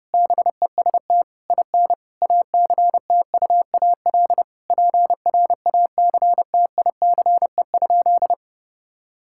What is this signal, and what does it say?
Morse code, heard as a keyed tone, about 32 wpm, 700 Hz. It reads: BEST IN ACTUAL PRACTICE?
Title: Sentences from Top 1000 Words - Review Part 5 - 30wpm